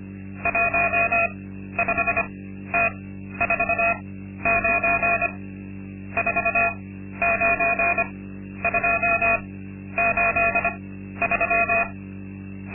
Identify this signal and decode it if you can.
Morse code, heard as a keyed tone, about 25 wpm, spaced as for 12 wpm, 680 Hz. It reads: 15T49 49283
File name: Linksmakalnis-2